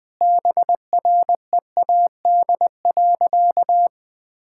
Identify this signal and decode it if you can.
Morse code, heard as a keyed tone, about 20 wpm, 700 Hz. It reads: BREAD.
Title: Sentences from Top 1600 Words 20wpm